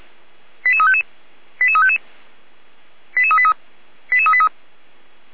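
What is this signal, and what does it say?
Five-tone selective call (ZVEI), heard as quick runs of five tones, tones of about 70 ms. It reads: 88355 88355 88383 88383
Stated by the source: ZVEI five-tone selective call received on an SDR.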